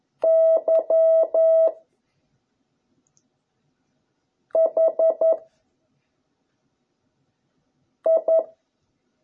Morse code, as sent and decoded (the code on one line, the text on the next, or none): -.-- / .... / ..
Y H I